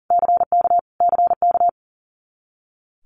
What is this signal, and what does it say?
Morse code, heard as a keyed tone, about 40 wpm, 700 Hz. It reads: CK CK
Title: Multi-letter Phonemes Repeated 3x 40wpm